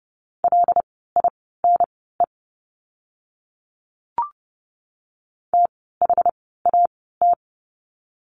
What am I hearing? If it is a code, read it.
Morse code, heard as a keyed tone, about 30 wpm, spaced as for 17 wpm, 700 Hz. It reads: LINE THAT
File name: course / 30:10wpm / 115 Lesson 19f - Letter w - Words 30wpm